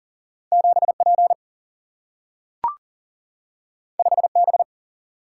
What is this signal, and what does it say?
Morse code, heard as a keyed tone, about 40 wpm, 700 Hz. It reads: 7P 5B
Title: Single-Letter Single-Number 40wpm